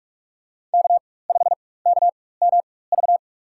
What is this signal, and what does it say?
Morse code, heard as a keyed tone, about 42 wpm, spaced as for 20 wpm, 700 Hz. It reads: K5KMV
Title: USA Call Signs - 5 Characters - 4x Character Spacing - 45wpm